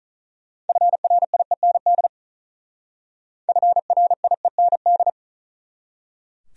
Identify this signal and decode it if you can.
Morse code, heard as a keyed tone, about 39 wpm, 700 Hz. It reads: FRIEND FRIEND
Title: Speed-Racing Top 500 Words 35wpm